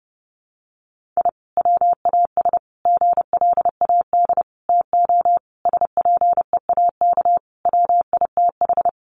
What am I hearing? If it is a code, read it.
Morse code, heard as a keyed tone, about 30 wpm, 700 Hz. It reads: I WAS GLAD TO SPEAK WITH
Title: Sentences from Top 1100 Words 30wpm